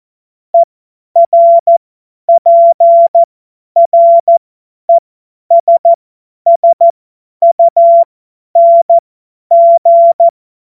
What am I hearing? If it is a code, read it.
Morse code, heard as a keyed tone, about 14 wpm, spaced as for 10 wpm, 680 Hz. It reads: ERPRESSUNG
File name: Morsekurs / Witze / Witz 27. Dezember 2022 50 F70